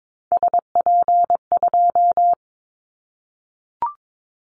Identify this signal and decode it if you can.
Morse code, heard as a keyed tone, about 22 wpm, 700 Hz. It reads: SP2
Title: Callsign Prefixes for Top 25 DXCC - 22wpm